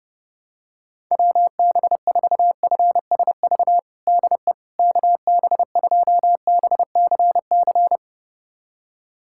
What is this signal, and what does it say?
Morse code, heard as a keyed tone, about 30 wpm, 700 Hz. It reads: WB4FSV DE KB2BCC
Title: USA Callsign DE Callsign - 6 Characters - 30wpm